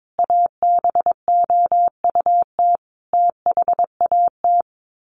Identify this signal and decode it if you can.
Morse code, heard as a keyed tone, about 22 wpm, 700 Hz. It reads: ABOUT THAT
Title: Sets of 5 Words Encore2 22wpm